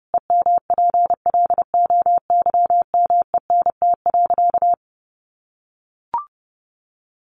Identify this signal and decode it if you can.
Morse code, heard as a keyed tone, about 30 wpm, 700 Hz. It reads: EMPLOYMENT.
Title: Sentences from Top 1500 Words 30wpm